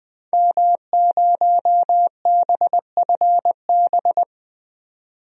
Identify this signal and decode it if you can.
Morse code, heard as a keyed tone, about 20 wpm, 700 Hz. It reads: M0BFB
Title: DXCC-100 Call Signs - 5 Characters - 20wpm